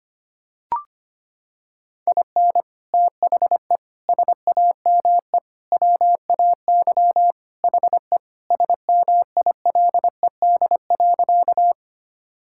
Morse code, heard as a keyed tone, about 25 wpm, 700 Hz. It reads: IN THE SAME WAY HE SMILED.
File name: practice / 25wpm / Sentences with 4-Word Phrases - 25wpm